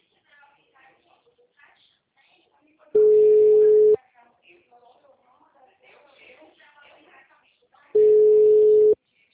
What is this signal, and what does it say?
Ringback tone (Europe, 425 Hz), with speech.
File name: out- -2046- - - .14778